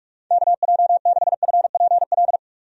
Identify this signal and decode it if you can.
Morse code, heard as a keyed tone, about 46 wpm, 700 Hz. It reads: KJ6FPL
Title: USA Call Signs - 6 Characters - 45wpm